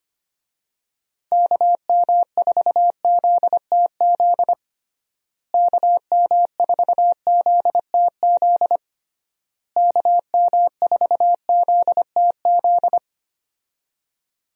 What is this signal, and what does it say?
Morse code, heard as a keyed tone, about 25 wpm, 700 Hz. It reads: KM4ZTZ KM4ZTZ KM4ZTZ
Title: USA Call Signs - 6 Characters - Repeated 3x - 25wpm